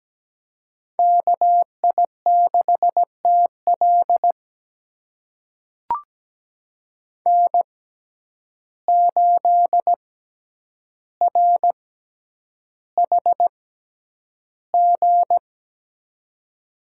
Morse code, heard as a keyed tone, about 17 wpm, spaced as for 9 wpm, 700 Hz. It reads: KI6TL N8RHG
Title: USA Call Signs - 5 Characters - 6x Character Spacing - 17wpm